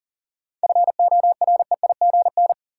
Morse code, heard as a keyed tone, about 40 wpm, 700 Hz. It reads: FOREIGN
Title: Sentences from Top 1400 Words - New Word Drill - 40wpm